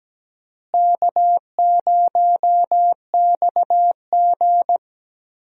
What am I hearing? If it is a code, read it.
Morse code, heard as a keyed tone, about 17 wpm, 700 Hz. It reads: K0XG
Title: USA Call Signs - 4 Characters - Repeated 3x - 17wpm